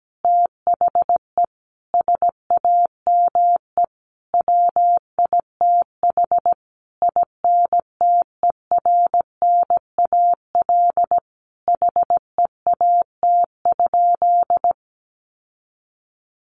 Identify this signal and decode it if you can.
Morse code, heard as a keyed tone, about 17 wpm, 700 Hz. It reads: THE SAME WITH INTERNAL HEAT?
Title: Sentences from Top 1400 Words 17wpm